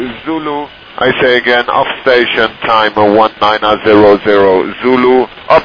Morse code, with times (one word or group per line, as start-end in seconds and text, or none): none